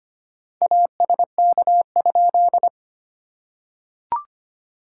Morse code, heard as a keyed tone, about 25 wpm, 690 Hz. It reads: ASK?
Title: SOTA QSO Practice 25wpm